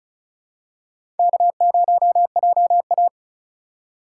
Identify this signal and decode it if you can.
Morse code, heard as a keyed tone, about 36 wpm, 700 Hz. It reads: K0JA
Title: USA Call Signs - 3x Character Spacing - 35wpm